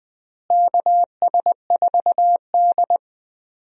Morse code, heard as a keyed tone, about 20 wpm, 700 Hz. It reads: KS4D
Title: USA Call Signs - 4 Characters - Repeated 3x - 20wpm